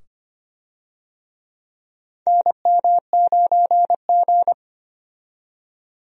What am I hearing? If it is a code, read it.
Morse code, heard as a keyed tone, about 25 wpm, 700 Hz. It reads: NM9G
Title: USA Call Signs - 4 Characters - 25wpm